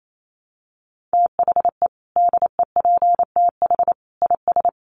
Morse code, heard as a keyed tone, about 28 wpm, 700 Hz. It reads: THE DEPTH IS
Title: Sentences from Top 1200 Words 28wpm